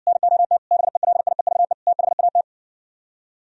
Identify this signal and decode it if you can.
Morse code, heard as a keyed tone, about 60 wpm, 700 Hz. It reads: NOT BELIEVE THAT